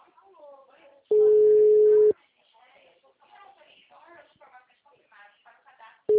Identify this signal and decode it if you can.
Ringback tone (Europe, 425 Hz), with speech.